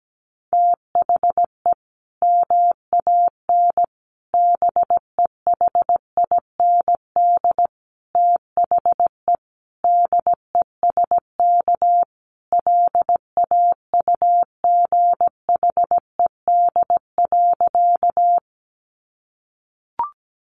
Morse code, heard as a keyed tone, about 17 wpm, 700 Hz. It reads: THE MAN BEHIND THE DESK LAUGHED.
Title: A Sound Of Thunder 17wpm